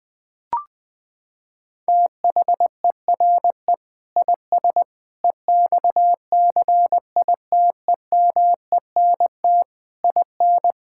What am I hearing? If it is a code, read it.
Morse code, heard as a keyed tone, about 20 wpm, 700 Hz. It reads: THERE IS EXCITEMENT IN
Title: Sentences with 3-Word Phrases - 20wpm